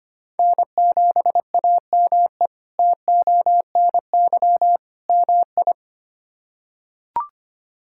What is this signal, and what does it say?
Morse code, heard as a keyed tone, about 25 wpm, 700 Hz. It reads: N7AME TONY MI
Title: K1USN SST QSO Practice - 25wpm